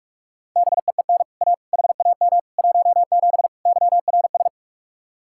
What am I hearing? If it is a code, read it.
Morse code, heard as a keyed tone, about 44 wpm, 700 Hz. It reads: BEEN A HAM 17 YRS